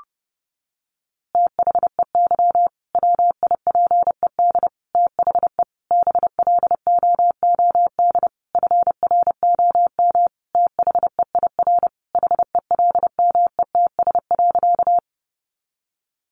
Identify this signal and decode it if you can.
Morse code, heard as a keyed tone, about 30 wpm, 700 Hz. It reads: THEY WIPED THE BLOOD FROM THEIR HELMETS.
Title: A Sound Of Thunder 30wpm